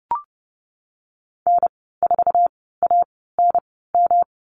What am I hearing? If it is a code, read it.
Morse code, heard as a keyed tone, about 31 wpm, spaced as for 17 wpm, 700 Hz. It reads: N4ANM
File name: Lesson 19g - Letter w - Call Signs 30wpm